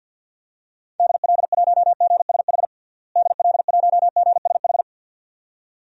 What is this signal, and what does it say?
Morse code, heard as a keyed tone, about 47 wpm, 700 Hz. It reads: DL1GSH DL1GSH
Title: DXCC-100 Call Signs - 6 Characters - Repeated 2x - 50wpm